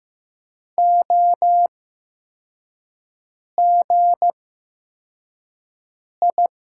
Morse code, heard as a keyed tone, about 15 wpm, spaced as for 4 wpm, 700 Hz. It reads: OGI
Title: USA Call Signs - 6 Characters - 8x Character Spacing - 15wpm